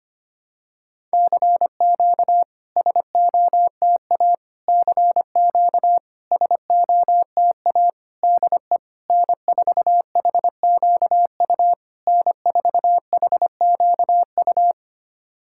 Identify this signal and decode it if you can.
Morse code, heard as a keyed tone, about 25 wpm, 700 Hz. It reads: CQ SOTA CQ SOTA DE N4HQU N4HQU
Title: SOTA QSO Practice 25wpm